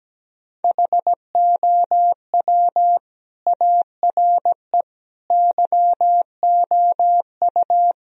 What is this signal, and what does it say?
Morse code, heard as a keyed tone, about 18 wpm, 700 Hz. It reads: HOW ARE YOU
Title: Speed-Racing - Sets of 5 Words 17wpm